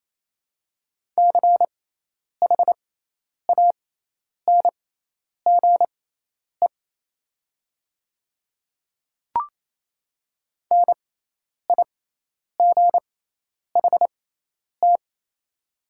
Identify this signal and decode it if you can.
Morse code, heard as a keyed tone, about 27 wpm, spaced as for 10 wpm, 690 Hz. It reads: CHANGE NIGHT
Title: 6x Character Spacing - Top 300 Words - 28wpm